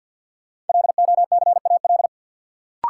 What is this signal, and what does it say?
Morse code, heard as a keyed tone, about 52 wpm, 700 Hz. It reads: LOCAL
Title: Sentences from Top 1500 Words - New Word Drill Alt - 50wpm